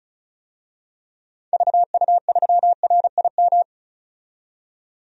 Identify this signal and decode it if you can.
Morse code, heard as a keyed tone, about 36 wpm, 700 Hz. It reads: VU3RIM
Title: DXCC-100 Call Signs - 6 Characters - 35wpm